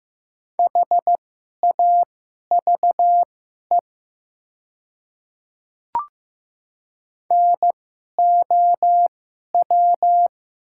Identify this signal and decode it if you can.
Morse code, heard as a keyed tone, about 15 wpm, spaced as for 11 wpm, 700 Hz. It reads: HAVE NOW
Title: 2x Character Spacing - Top 100 Words - 15wpm